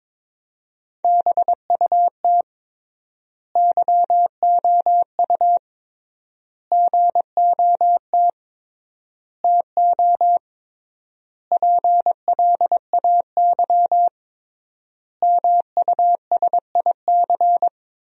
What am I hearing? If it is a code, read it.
Morse code, heard as a keyed tone, about 22 wpm, 700 Hz. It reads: BUT YOU GOT TO PLAY MUSIC
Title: 3x Word-Spacing - Sentences from Top 300 Words - 22wpm